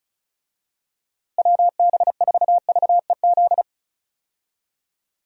Morse code, heard as a keyed tone, about 34 wpm, 700 Hz. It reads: WB4VEZ
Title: USA Call Signs - 6 Characters - Repeated 2x - 35wpm